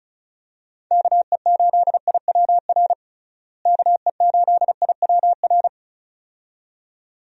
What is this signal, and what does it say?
Morse code, heard as a keyed tone, about 36 wpm, 700 Hz. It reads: KE8IWR KE8IWR
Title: USA Call Signs - 6 Characters - Repeated 2x - 35wpm